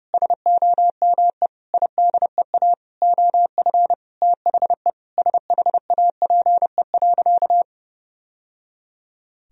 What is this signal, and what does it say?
Morse code, heard as a keyed tone, about 30 wpm, 700 Hz. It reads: SOME IDEA OF THE SHAPE.
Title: Sentences from Top 1000 Words - Review Part 3 - 30wpm